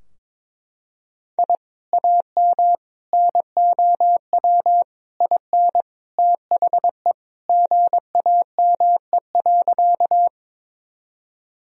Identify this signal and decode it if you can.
Morse code, heard as a keyed tone, about 22 wpm, 700 Hz. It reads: I AM NOW IN THE GAME.
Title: Sentences from Top 1000 Words - Review Part 3 - 22wpm